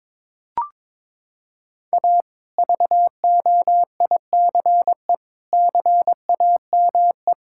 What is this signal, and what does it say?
Morse code, heard as a keyed tone, about 22 wpm, 700 Hz. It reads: A VOICE CAME
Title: Sentences from Top 1000 Words - Review Part 5 - 22wpm